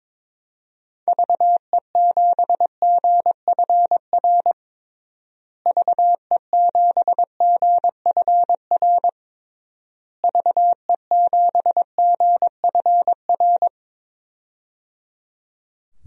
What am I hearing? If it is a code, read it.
Morse code, heard as a keyed tone, about 22 wpm, 700 Hz. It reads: VE7GFR VE7GFR VE7GFR
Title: DXCC-100 Call Signs - 6 Characters - Repeated 3x - 22wpm